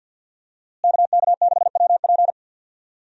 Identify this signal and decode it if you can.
Morse code, heard as a keyed tone, about 48 wpm, 700 Hz. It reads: KK6WP
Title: USA Call Signs - 5 Characters - Repeated 3x - 50wpm